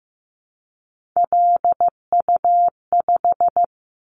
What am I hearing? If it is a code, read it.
Morse code, heard as a keyed tone, about 15 wpm, 700 Hz. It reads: LU5
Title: Callsign Prefixes for Top 100 DXCC - 15wpm